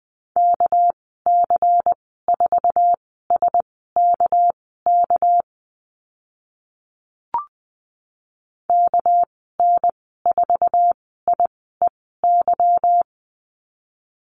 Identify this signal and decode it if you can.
Morse code, heard as a keyed tone, about 20 wpm, spaced as for 15 wpm, 700 Hz. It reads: KC4SKK KN4IEY
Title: 200 Lesson 32g - Letter k - Call Signs 20wpm